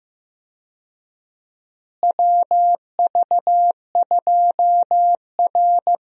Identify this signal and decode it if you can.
Morse code, heard as a keyed tone, about 15 wpm, 700 Hz. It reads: WV2R
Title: USA Call Signs - 4 Characters - Repeated 2x - 15wpm